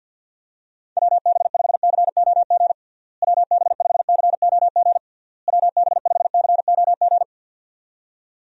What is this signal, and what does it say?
Morse code, heard as a keyed tone, about 51 wpm, 700 Hz. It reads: WB5COG WB5COG WB5COG
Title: USA Call Signs - 6 Characters - Repeated 3x - 50wpm